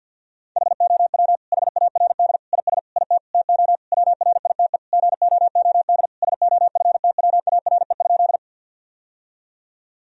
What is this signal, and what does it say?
Morse code, heard as a keyed tone, about 50 wpm, 700 Hz. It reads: HOW HARD IS IT TO WRITE GOOD SOFTWARE?
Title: Sentences from Top 1000 Words - Review Part 3 - 50wpm